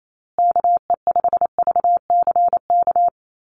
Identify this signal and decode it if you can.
Morse code, heard as a keyed tone, about 28 wpm, 700 Hz. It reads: KE5VCK
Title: USA Call Signs Repeated 2x 28wpm